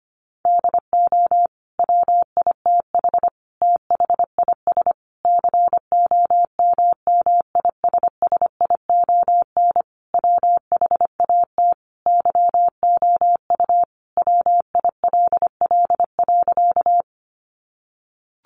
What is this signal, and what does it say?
Morse code, heard as a keyed tone, about 25 wpm, 700 Hz. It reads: DO WITH THIS COMMISSION WHAT YOU WILL.
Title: Sentences from Top 1400 Words 25wpm